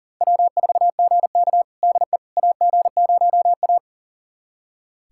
Morse code, heard as a keyed tone, about 37 wpm, 700 Hz. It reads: W4GK DE AG0A
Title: USA Callsign DE Callsign - 4 Characters - 40wpm